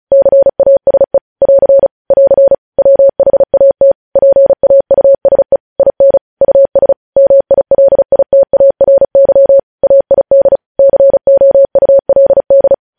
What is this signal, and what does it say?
Morse code, heard as a keyed tone, about 36 wpm, 550 Hz. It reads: CASE <AR> <AR> WHAT PAUSE IN US MILITARY AID COULD
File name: podcast / News headlines-1x-35x35